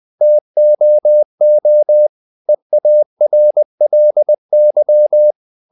Morse code, heard as a keyed tone, about 20 wpm, 600 Hz. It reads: TOO EARLY
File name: the old man and the sea part 2 20wpm